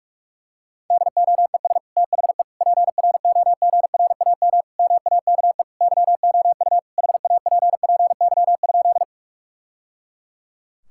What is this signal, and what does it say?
Morse code, heard as a keyed tone, about 46 wpm, 700 Hz. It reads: DOES THE PROGRAM MAKE YOU HAPPY?
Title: Sentences from Top 1000 Words - Review Part 5 - 45wpm